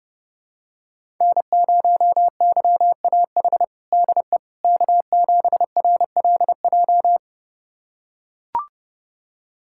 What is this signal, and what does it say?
Morse code, heard as a keyed tone, about 32 wpm, 700 Hz. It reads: N0YAH DE K7RLJ
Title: USA Callsign DE Callsign - 5 Characters - 30wpm